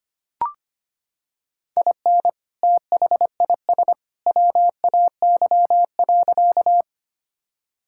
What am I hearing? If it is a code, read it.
Morse code, heard as a keyed tone, about 25 wpm, 700 Hz. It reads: IN THIS WAY.
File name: practice / 25wpm / Sentences with 2-Word Phrases - 25wpm